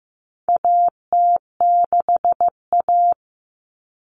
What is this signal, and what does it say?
Morse code, heard as a keyed tone, about 15 wpm, 700 Hz. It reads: AT6A
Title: Common Cut Numbers - 4-Digit 15wpm